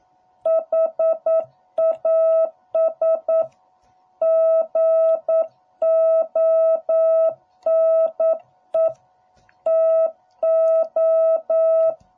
Morse code, heard as a keyed tone, about 9 wpm, 660 Hz. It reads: HAS GONE TO